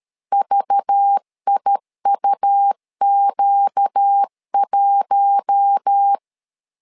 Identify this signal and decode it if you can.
Morse code, heard as a keyed tone, about 13 wpm, 800 Hz. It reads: VIUQ1
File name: LEZIONI / 5 LBVC / 5 LBV 12wpm 1855